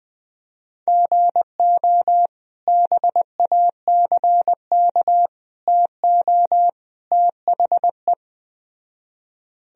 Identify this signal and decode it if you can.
Morse code, heard as a keyed tone, about 21 wpm, 700 Hz. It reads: GO BACK TO THE